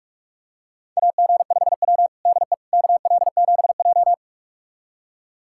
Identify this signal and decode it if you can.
Morse code, heard as a keyed tone, about 41 wpm, 700 Hz. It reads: AG5W DE KL7J
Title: USA Callsign DE Callsign - 4 Characters - 45wpm